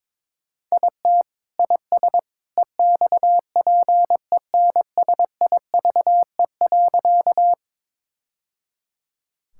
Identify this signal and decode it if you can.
Morse code, heard as a keyed tone, about 22 wpm, 700 Hz. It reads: IT IS EXPENSIVE.
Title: Sentences from Top 1000 Words - Review Part 6 - 22wpm